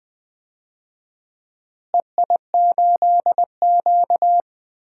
Morse code, heard as a keyed tone, about 20 wpm, 700 Hz. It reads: EI8Q
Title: DXCC-100 Call Signs - 4 Characters - Repeated 2x - 20wpm